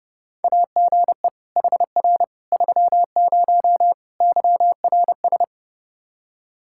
Morse code, heard as a keyed tone, about 32 wpm, 700 Hz. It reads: AGE HR 30 YRS
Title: Practice QSO - Exchange 3 - More Info - 30wpm